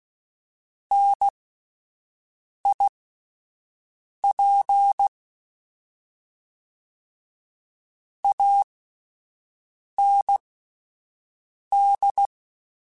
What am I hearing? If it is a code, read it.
Morse code, heard as a keyed tone, about 16 wpm, spaced as for 5 wpm, 800 Hz. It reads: NIP AND
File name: Misc / Misc-mp3s / WordPairs sp @10